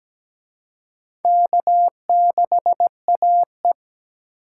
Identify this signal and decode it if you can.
Morse code, heard as a keyed tone, about 21 wpm, 700 Hz. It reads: K6AE
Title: Speed Racing - USA 1x2 Call Signs 17wpm